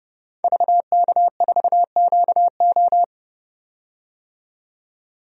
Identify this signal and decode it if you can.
Morse code, heard as a keyed tone, about 32 wpm, 700 Hz. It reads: VK4QO
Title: DXCC-100 Call Signs - 5 Characters - 30wpm